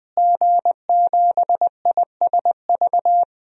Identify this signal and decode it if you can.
Morse code, heard as a keyed tone, about 20 wpm, 700 Hz. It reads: G7ISV
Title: DXCC-100 Call Signs - 5 Characters - Repeated 2x - 20wpm